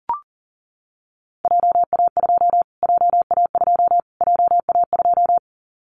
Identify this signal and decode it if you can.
Morse code, heard as a keyed tone, about 40 wpm, 700 Hz. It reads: JA2 JA2 JA2
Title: Callsign Prefixes for Top 25 DXCC - Repeated 3x - 40wpm